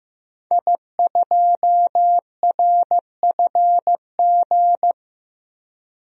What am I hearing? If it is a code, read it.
Morse code, heard as a keyed tone, about 15 wpm, 700 Hz. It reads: I2RFG
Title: DXCC-100 Call Signs - 5 Characters - 15wpm